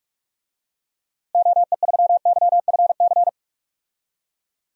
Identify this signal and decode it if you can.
Morse code, heard as a keyed tone, about 42 wpm, 700 Hz. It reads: OE3YFC
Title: DXCC-100 Call Signs - 6 Characters - Repeated 2x - 45wpm